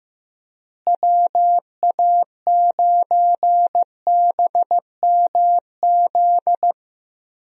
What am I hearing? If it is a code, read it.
Morse code, heard as a keyed tone, about 15 wpm, 700 Hz. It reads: WA9BMZ
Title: USA Call Signs - 6 Characters - Repeated 2x - 15wpm